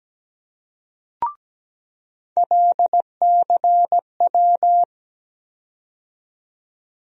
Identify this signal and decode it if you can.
Morse code, heard as a keyed tone, about 17 wpm, 700 Hz. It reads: LCW
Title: USA Call Sign Postfixes - 3 Letters 17wpm